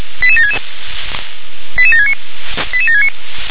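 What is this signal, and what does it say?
Five-tone selective call (ZVEI), heard as quick runs of five tones, tones of about 70 ms. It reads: 88766 88769 88769
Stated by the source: ZVEI five-tone selective call received on an SDR.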